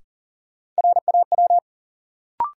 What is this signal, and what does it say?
Morse code, heard as a keyed tone, about 40 wpm, 700 Hz. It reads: RAW
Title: Sentences from Top 1500 Words - New Word Drill Alt - 40wpm